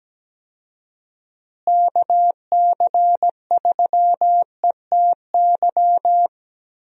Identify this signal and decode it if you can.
Morse code, heard as a keyed tone, about 17 wpm, 700 Hz. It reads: KC3ETY
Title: USA Call Signs - 6 Characters - Repeated 2x - 17wpm